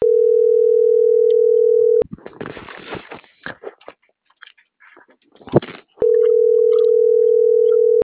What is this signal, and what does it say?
Ringback tone (North America), 440 + 480 Hz.
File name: CENTRAL MAERSK / out luis.herrera - .26818